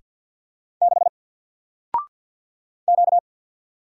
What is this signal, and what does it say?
Morse code, heard as a keyed tone, about 50 wpm, 700 Hz. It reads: 6 Q
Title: Single Letter-Number 50wpm